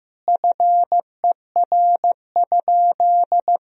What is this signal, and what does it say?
Morse code, heard as a keyed tone, about 15 wpm, 700 Hz. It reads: FER?
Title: Sentences from Top 100 Words and Abbreviations 15wpm